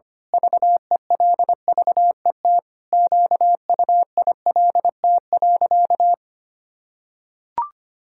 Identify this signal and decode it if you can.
Morse code, heard as a keyed tone, about 25 wpm, 700 Hz. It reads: VELVET QUILT.
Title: Pangram Challenge v2 - 25wpm